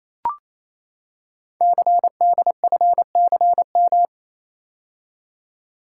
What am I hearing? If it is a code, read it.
Morse code, heard as a keyed tone, about 29 wpm, 700 Hz. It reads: CDFCM
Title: Code Groups - 5 Letters - 28wpm